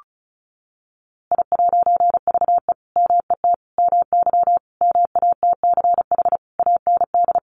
Morse code, heard as a keyed tone, about 34 wpm, 700 Hz. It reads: I'VE MET MY MATCH AND
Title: A Sound Of Thunder 35wpm